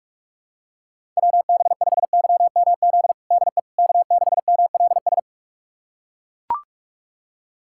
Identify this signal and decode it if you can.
Morse code, heard as a keyed tone, about 41 wpm, 690 Hz. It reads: WB5YMZ DE K6MLS
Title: USA Callsign DE Callsign 45wpm